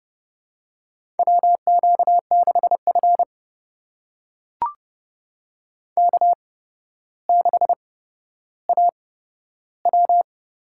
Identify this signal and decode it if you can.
Morse code, heard as a keyed tone, about 28 wpm, spaced as for 13 wpm, 700 Hz. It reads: WQ6F K6AW
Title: USA Call Signs - 4 Characters - 8x Character Spacing - 30wpm